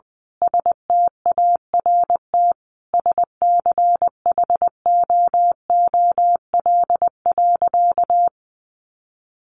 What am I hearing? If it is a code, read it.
Morse code, heard as a keyed tone, about 20 wpm, 700 Hz. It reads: START SCHOOL.
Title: Sentences from Top 1100 Words 20wpm